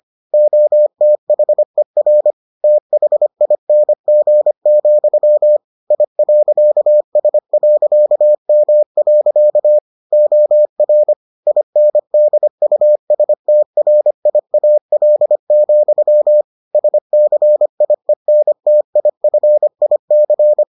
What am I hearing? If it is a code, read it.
Morse code, heard as a keyed tone, about 25 wpm, 600 Hz. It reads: OTHER THING, I.S.M. OR INDUSTRIAL, SCIENTIFIC